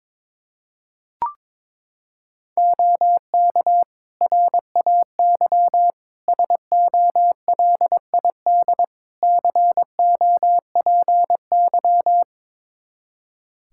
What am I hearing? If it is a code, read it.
Morse code, heard as a keyed tone, about 22 wpm, 700 Hz. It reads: OK RAY SOLID COPY
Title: Practice QSO - Exchange 3 - More Info - 22wpm